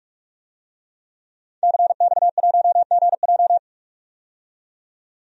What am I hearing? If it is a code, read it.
Morse code, heard as a keyed tone, about 46 wpm, 700 Hz. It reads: CX1GJ
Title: DXCC-100 Call Signs - 5 Characters - 45wpm